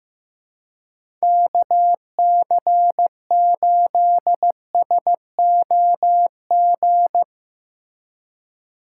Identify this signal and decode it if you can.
Morse code, heard as a keyed tone, about 15 wpm, 700 Hz. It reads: KC8SOG
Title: USA Call Signs - 2x Character Spacing - 15wpm